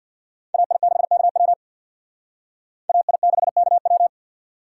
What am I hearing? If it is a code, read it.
Morse code, heard as a keyed tone, about 62 wpm, 700 Hz. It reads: AI6KW AI6KW
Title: Speed Racing - USA Call Signs 50wpm